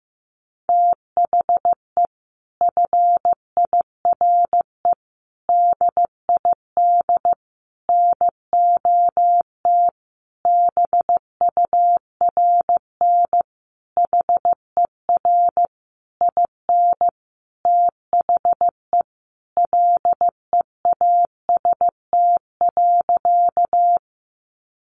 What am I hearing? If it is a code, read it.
Morse code, heard as a keyed tone, about 15 wpm, 700 Hz. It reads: THE FIRE DID NOT BURN HER IN THE LEAST.
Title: Sentences from Top 1300 Words 15wpm